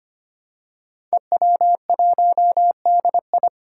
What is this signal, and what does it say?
Morse code, heard as a keyed tone, about 25 wpm, 700 Hz. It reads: EW1DI
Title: Five Challenging Letter-Number Combinations 25wpm